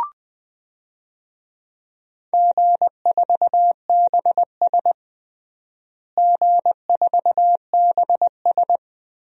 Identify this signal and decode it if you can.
Morse code, heard as a keyed tone, about 21 wpm, 700 Hz. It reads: G4BS G4BS
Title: DXCC-100 Call Signs - 4 Characters - Repeated 3x - 20wpm